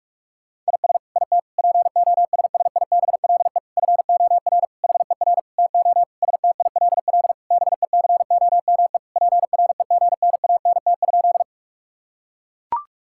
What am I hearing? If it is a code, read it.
Morse code, heard as a keyed tone, about 41 wpm, 690 Hz. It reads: IS IT POSSIBLE FOR HER TO STILL BECOME PREGNANT?
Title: Sentences from Top 1000 Words - Review Part 6 - 45wpm